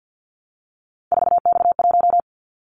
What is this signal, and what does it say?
Morse code, heard as a keyed tone, about 50 wpm, 700 Hz. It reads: 4X1
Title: Callsign Prefixes for Top 50 DXCC - 50wpm